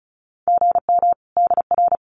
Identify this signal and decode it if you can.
Morse code, heard as a keyed tone, about 34 wpm, 700 Hz. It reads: GM DR
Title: Common QSO Phrases 35wpm